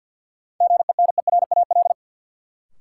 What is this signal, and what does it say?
Morse code, heard as a keyed tone, about 53 wpm, 700 Hz. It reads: GENERAL
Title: Sentences from Top 600 Words - New Word Drill - 50wpm